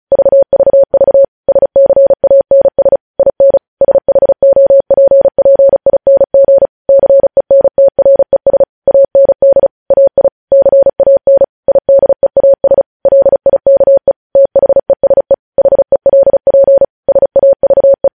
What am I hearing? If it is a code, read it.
Morse code, heard as a keyed tone, about 36 wpm, 550 Hz. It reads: VVV SCANS IN SHOPPING CENTRES AND AI CAN IDEAS LIKE THESE HELP SAVE
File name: podcast / News headlines-1x-35x35